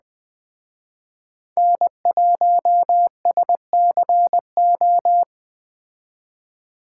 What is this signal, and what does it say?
Morse code, heard as a keyed tone, about 21 wpm, 700 Hz. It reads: N1SCO